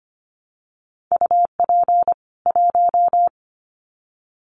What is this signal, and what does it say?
Morse code, heard as a keyed tone, about 25 wpm, 700 Hz. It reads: UP 1
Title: Common QSO Phrases 25wpm